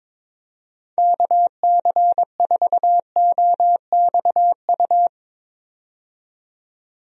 Morse code, heard as a keyed tone, about 22 wpm, 700 Hz. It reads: KC4OXU
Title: USA Call Signs - 6 Characters - 22wpm